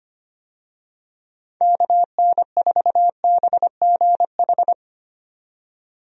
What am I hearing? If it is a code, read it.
Morse code, heard as a keyed tone, about 25 wpm, 700 Hz. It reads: KN4BGH
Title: USA Call Signs - 6 Characters - 25wpm